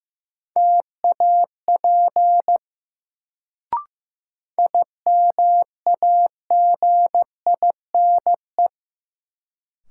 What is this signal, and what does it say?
Morse code, heard as a keyed tone, about 15 wpm, 700 Hz. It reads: TAP IMAGINE
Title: Sentences from Top 1400 Words - New Word Drill Alt - 15wpm